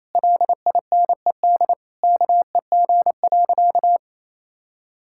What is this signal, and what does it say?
Morse code, heard as a keyed tone, about 28 wpm, 700 Hz. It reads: LINED KEG.
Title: Pangram Challenge v2 - 28wpm